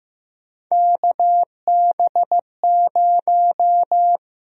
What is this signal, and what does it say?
Morse code, heard as a keyed tone, about 15 wpm, 700 Hz. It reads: KB0
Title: USA Call Sign Prefixes - Repeated 2x - 15wpm